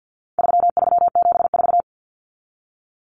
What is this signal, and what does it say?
Morse code, heard as a keyed tone, about 49 wpm, 700 Hz. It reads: 3374
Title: Numbers - 4-Digit 50wpm